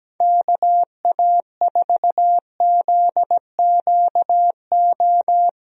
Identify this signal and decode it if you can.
Morse code, heard as a keyed tone, about 17 wpm, 700 Hz. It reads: KA4ZQO
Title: USA Call Signs - 6 Characters - Repeated 3x - 17wpm